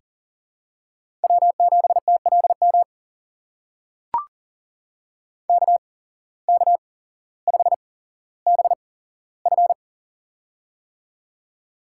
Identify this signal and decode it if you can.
Morse code, heard as a keyed tone, about 43 wpm, spaced as for 17 wpm, 700 Hz. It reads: W7TLM KK5BF
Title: USA Call Signs - 5 Characters - 8x Character Spacing - 40wpm